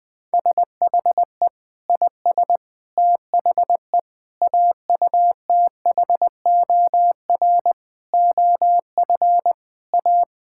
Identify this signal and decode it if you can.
Morse code, heard as a keyed tone, about 20 wpm, 700 Hz. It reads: SHE IS THE AUTHOR OF A
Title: Sentences with 4-Word Phrases - 20wpm